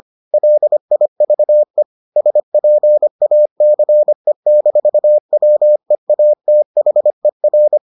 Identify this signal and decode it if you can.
Morse code, heard as a keyed tone, about 25 wpm, 600 Hz. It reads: LIVE SPACE-WEATHER